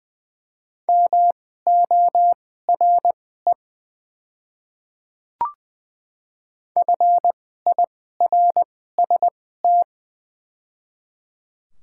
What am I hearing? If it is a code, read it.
Morse code, heard as a keyed tone, about 20 wpm, spaced as for 15 wpm, 700 Hz. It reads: MORE FIRST
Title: 2x Character Spacing - Top 200 Words - 20wpm